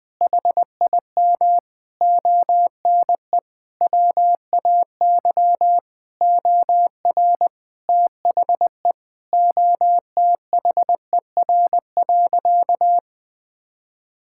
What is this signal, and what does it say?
Morse code, heard as a keyed tone, about 20 wpm, 700 Hz. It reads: HIM ONE WAY OR THE OTHER.
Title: Sentences from Top 1000 Words - Review Part 6 - 20wpm